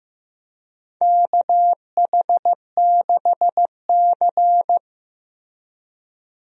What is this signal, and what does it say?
Morse code, heard as a keyed tone, about 15 wpm, 700 Hz. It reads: KH6C
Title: DXCC-100 Call Signs - 4 Characters - Repeated 3x - 15wpm